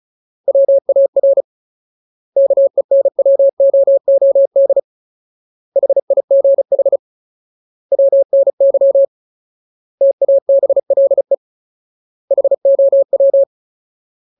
Morse code, heard as a keyed tone, about 34 wpm, 550 Hz. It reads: WAR KENWOOD HIGH WNY TABLE HOW
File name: Ultimate Word List / 35WPM Ultimate